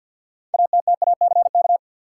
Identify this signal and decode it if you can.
Morse code, heard as a keyed tone, about 51 wpm, 700 Hz. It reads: ATTACK
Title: Sentences from Top 1500 Words - New Word Drill Alt - 50wpm